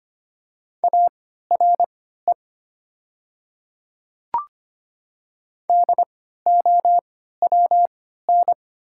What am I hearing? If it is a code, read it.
Morse code, heard as a keyed tone, about 25 wpm, spaced as for 14 wpm, 700 Hz. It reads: ARE DOWN
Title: 3x Character Spacing - Top 100 Words - 25wpm